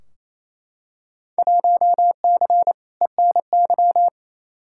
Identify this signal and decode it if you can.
Morse code, heard as a keyed tone, about 28 wpm, 700 Hz. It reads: 1C ENY
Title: Field Day Exchanges Encore - 28wpm